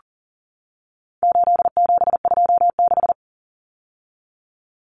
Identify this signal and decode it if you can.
Morse code, heard as a keyed tone, about 40 wpm, 700 Hz. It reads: 8726
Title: Common Cut Numbers - 4-Digit 40wpm